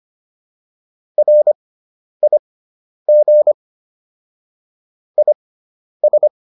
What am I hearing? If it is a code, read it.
Morse code, heard as a keyed tone, about 25 wpm, spaced as for 10 wpm, 600 Hz. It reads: RIG IS
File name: QSO404 10